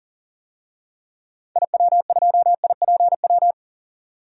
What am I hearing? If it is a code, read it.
Morse code, heard as a keyed tone, about 40 wpm, 700 Hz. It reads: IW2IPW
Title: DXCC-100 Call Signs - 6 Characters - 40wpm